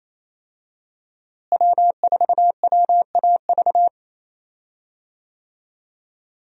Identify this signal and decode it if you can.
Morse code, heard as a keyed tone, about 28 wpm, 700 Hz. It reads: W4WAV
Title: USA Call Signs - 5 Characters - Repeated 2x - 28wpm